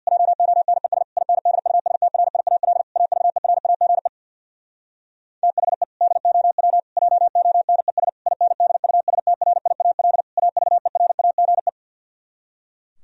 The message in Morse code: JONES INDUSTRIAL AVERAGE THE DOW JONES INDUSTRIAL AVERAGE

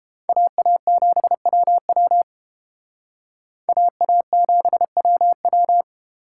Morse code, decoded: AA7WW AA7WW

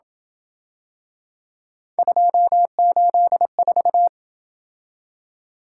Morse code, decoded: 284

